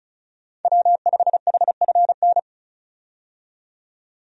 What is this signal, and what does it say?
Morse code, heard as a keyed tone, about 36 wpm, 700 Hz. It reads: W5HFN
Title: USA Call Signs - 5 Characters - 35wpm